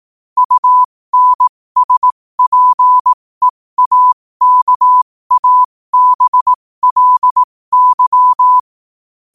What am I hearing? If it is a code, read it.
Morse code, heard as a keyed tone, about 18 wpm, spaced as for 15 wpm, 1000 Hz. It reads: UNSPEAKABLY
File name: qotdfiles / 15wpm